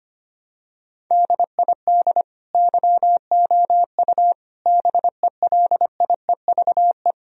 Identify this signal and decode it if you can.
Morse code, heard as a keyed tone, about 25 wpm, 700 Hz. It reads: DID YOU BELIEVE
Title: Sentences from Top 1000 Words - Review Part 1 - 25wpm